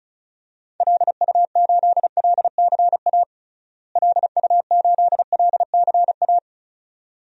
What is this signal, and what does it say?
Morse code, heard as a keyed tone, about 34 wpm, 700 Hz. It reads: LU8LCA LU8LCA
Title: DXCC-100 Call Signs - 6 Characters - Repeated 2x - 35wpm